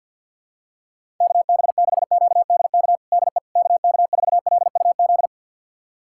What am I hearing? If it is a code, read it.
Morse code, heard as a keyed tone, about 46 wpm, 700 Hz. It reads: KB6QDK DE KK4LUZ
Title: USA Callsign DE Callsign 50wpm